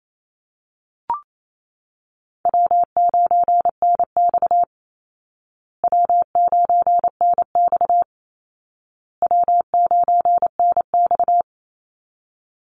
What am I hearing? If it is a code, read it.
Morse code, heard as a keyed tone, about 28 wpm, 700 Hz. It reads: W9NX W9NX W9NX
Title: Easy Call Signs Repeated 3x - 28wpm